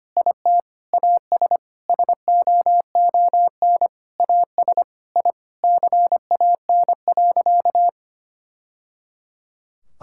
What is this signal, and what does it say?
Morse code, heard as a keyed tone, about 25 wpm, 700 Hz. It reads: IT AS SOON AS I CAN.
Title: Sentences from Top 1000 Words - Review Part 5 - 25wpm